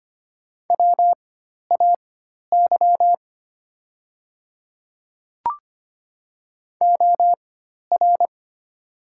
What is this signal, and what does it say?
Morse code, heard as a keyed tone, about 25 wpm, spaced as for 12 wpm, 700 Hz. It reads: WAY OR
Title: 4x Character Spacing - Top 100 Words - 25wpm